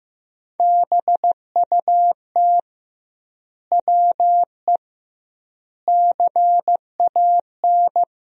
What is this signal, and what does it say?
Morse code, heard as a keyed tone, about 15 wpm, 700 Hz. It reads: BUT WE CAN
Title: 2x Word-Spacing - Sentences from Top 100 Words - 15wpm